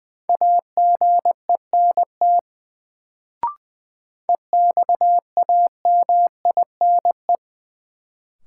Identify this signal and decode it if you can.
Morse code, heard as a keyed tone, about 21 wpm, 700 Hz. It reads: AGENT EXAMINE
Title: Sentences from Top 1600 Words - New Word Drill - 20wpm